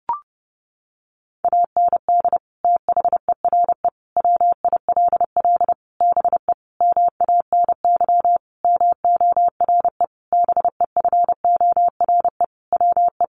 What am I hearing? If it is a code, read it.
Morse code, heard as a keyed tone, about 28 wpm, 700 Hz. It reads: AND THERE WILL BE MANY MORE BEFORE WE
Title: Sentences from Top 1500 Words 30wpm